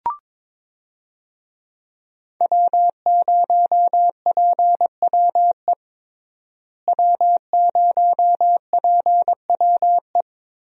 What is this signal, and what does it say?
Morse code, heard as a keyed tone, about 22 wpm, 700 Hz. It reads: W0PWE W0PWE
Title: USA Call Signs - 5 Characters - Repeated 3x - 22wpm